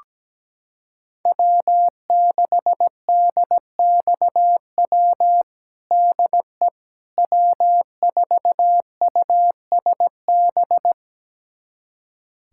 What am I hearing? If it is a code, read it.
Morse code, heard as a keyed tone, about 17 wpm, 700 Hz. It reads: W6DXW DE W4USB
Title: USA Callsign DE Callsign - 5 Characters - 17wpm